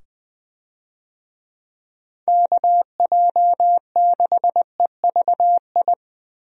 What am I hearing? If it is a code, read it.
Morse code, heard as a keyed tone, about 20 wpm, 700 Hz. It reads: KJ6EVI
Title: USA Call Signs - 6 Characters - 20wpm